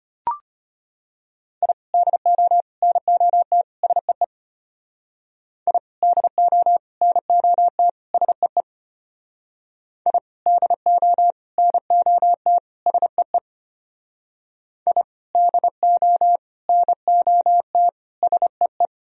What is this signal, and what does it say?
Morse code, heard as a keyed tone, about 33 wpm, 700 Hz. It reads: I DO NOT SEE I DO NOT SEE I DO NOT SEE I DO NOT SEE